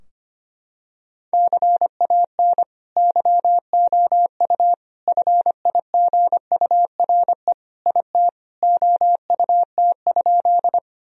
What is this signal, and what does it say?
Morse code, heard as a keyed tone, about 25 wpm, 700 Hz. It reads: CAN YOU FIGURE IT OUT?
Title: Sentences from Top 1000 Words - Review Part 2 - 25wpm